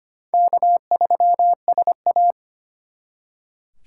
Morse code, heard as a keyed tone, about 32 wpm, 700 Hz. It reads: K3SA